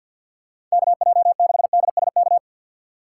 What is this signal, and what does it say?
Morse code, heard as a keyed tone, about 50 wpm, 700 Hz. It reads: KJ6DSK
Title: USA Call Signs - 6 Characters - 6x Character Spacing - 50wpm